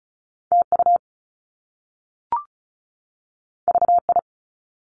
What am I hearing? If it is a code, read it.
Morse code, heard as a keyed tone, about 36 wpm, 700 Hz. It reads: TU VI
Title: Top Two Letter Combinations in Words 35wpm